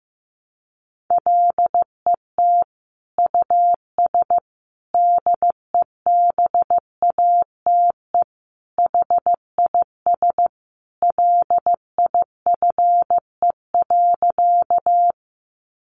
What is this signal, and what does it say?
Morse code, heard as a keyed tone, about 15 wpm, 700 Hz. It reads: LET US DEBATE HIS LIFE.
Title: Sentences from Top 1100 Words 15wpm